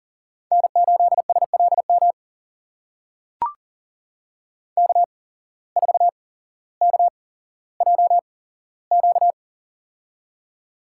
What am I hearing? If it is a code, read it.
Morse code, heard as a keyed tone, about 39 wpm, spaced as for 17 wpm, 700 Hz. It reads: N8SLM K4KJQ
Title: USA Call Signs - 5 Characters - 8x Character Spacing - 40wpm